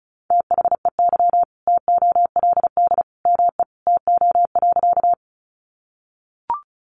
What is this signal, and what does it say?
Morse code, heard as a keyed tone, about 34 wpm, 700 Hz. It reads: THEY TOLD ME TO.
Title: Sentences from Top 400 Words 35wpm